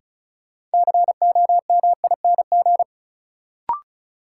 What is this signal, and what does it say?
Morse code, heard as a keyed tone, about 36 wpm, 700 Hz. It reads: COMING
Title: Sentences from Top 1300 Words - New Word Drill - 35wpm